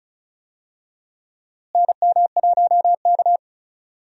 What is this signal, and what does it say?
Morse code, heard as a keyed tone, about 36 wpm, 700 Hz. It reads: NM1K
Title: USA Call Signs - 4 Characters - Repeated 3x - 35wpm